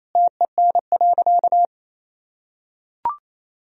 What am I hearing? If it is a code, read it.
Morse code, heard as a keyed tone, about 27 wpm, 700 Hz. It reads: TEN.
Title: Sentences from Top 1000 Words - Review Part 2 - 28wpm